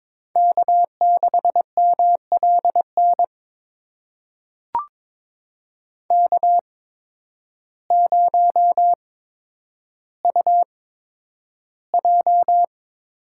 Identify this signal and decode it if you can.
Morse code, heard as a keyed tone, about 22 wpm, spaced as for 10 wpm, 700 Hz. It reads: K6MLN K0UJ
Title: USA Call Signs - 8x Character Spacing - 22wpm